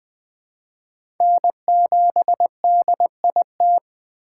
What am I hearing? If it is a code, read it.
Morse code, heard as a keyed tone, about 20 wpm, 700 Hz. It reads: N7DIT
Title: USA Call Signs - 5 Characters - 2x Character Spacing - 20wpm